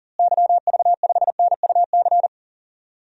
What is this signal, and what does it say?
Morse code, heard as a keyed tone, about 40 wpm, 700 Hz. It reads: YV5NUC
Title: DXCC-100 Call Signs - 6 Characters - Repeated 2x - 40wpm